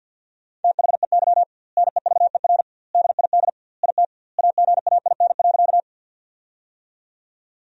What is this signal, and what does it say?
Morse code, heard as a keyed tone, about 50 wpm, 700 Hz. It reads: THEY NEVER DID IT AGAIN.
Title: 2x Word-Spacing - Sentences from Top 300 Words - 50wpm